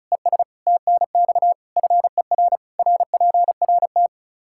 Morse code, heard as a keyed tone, about 36 wpm, 700 Hz. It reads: ES TNX FER RPRT
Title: Practice QSO - Exchange 1 - Beginning - 35wpm